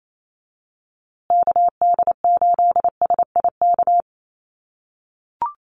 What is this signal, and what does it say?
Morse code, heard as a keyed tone, about 28 wpm, 700 Hz. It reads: KD8SIK
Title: USA Call Signs Repeated 3x 28wpm